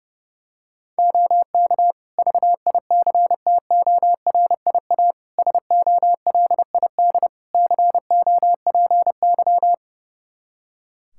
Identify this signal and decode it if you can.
Morse code, heard as a keyed tone, about 32 wpm, 700 Hz. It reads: OK VICTORIA SOLID COPY
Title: Practice QSO - Exchange 3 - More Info - 30wpm